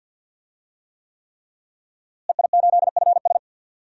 Morse code, heard as a keyed tone, about 50 wpm, 700 Hz. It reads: EI8FS
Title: DXCC-100 Call Signs - 5 Characters - Repeated 3x - 50wpm